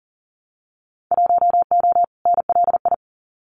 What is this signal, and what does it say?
Morse code, heard as a keyed tone, about 38 wpm, 700 Hz. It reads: 1O NLI
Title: Winter Field Day Exchanges 40wpm